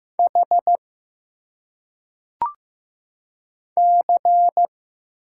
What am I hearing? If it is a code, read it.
Morse code, heard as a keyed tone, about 15 wpm, 700 Hz. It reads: H C
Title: Single Letters - Phonetic Pronunciation 15wpm